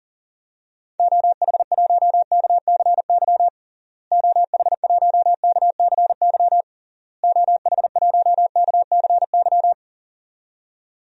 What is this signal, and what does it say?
Morse code, heard as a keyed tone, about 40 wpm, 700 Hz. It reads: OH1KCY OH1KCY OH1KCY
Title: DXCC-100 Call Signs - 6 Characters - Repeated 3x - 40wpm